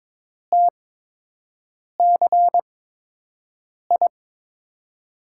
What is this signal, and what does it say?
Morse code, heard as a keyed tone, about 22 wpm, spaced as for 6 wpm, 700 Hz. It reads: TCI
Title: USA Call Signs - 6 Characters - 8x Character Spacing - 22wpm